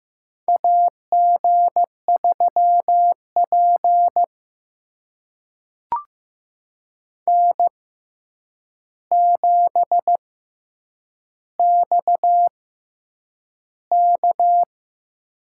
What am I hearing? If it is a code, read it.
Morse code, heard as a keyed tone, about 15 wpm, spaced as for 8 wpm, 700 Hz. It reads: AG3P N7XK
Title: USA Call Signs - 4 Characters - 6x Character Spacing - 15wpm